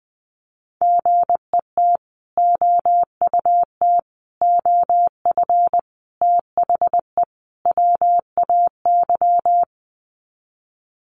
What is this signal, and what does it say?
Morse code, heard as a keyed tone, about 20 wpm, 700 Hz. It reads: GET OUT OF THE WAY
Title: Sets of 5 Words Encore 20wpm